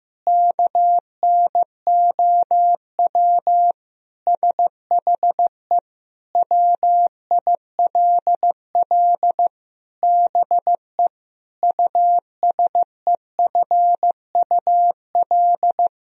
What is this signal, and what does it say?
Morse code, heard as a keyed tone, about 15 wpm, 700 Hz. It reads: KNOW SHE WILL BE USEFUL
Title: Sentences from Top 1000 Words - Review Part 4 - 15wpm